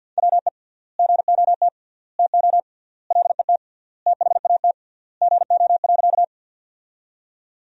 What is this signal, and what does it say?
Morse code, heard as a keyed tone, about 51 wpm, 700 Hz. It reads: WE GOT TO LET THAT GO.
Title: 3x Word-Spacing - Sentences from Top 300 Words - 50wpm